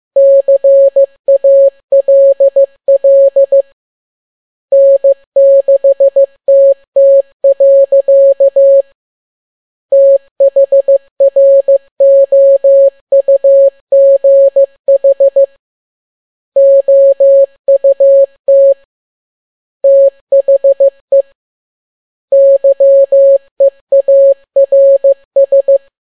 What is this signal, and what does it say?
Morse code, heard as a keyed tone, about 15 wpm, 550 Hz. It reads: CALL N6TT. THROUGH OUT THE YEARS